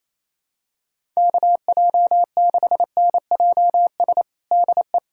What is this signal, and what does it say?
Morse code, heard as a keyed tone, about 27 wpm, 700 Hz. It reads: KJ6NJS DE
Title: USA Callsign DE Callsign 28wpm